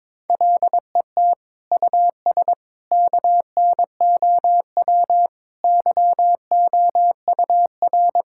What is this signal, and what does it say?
Morse code, heard as a keyed tone, about 22 wpm, 700 Hz. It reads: LET US KNOW YOUR
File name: practice / 22wpm / Sets of 8 Words - 22wpm